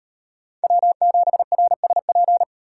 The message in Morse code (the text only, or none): W7RSP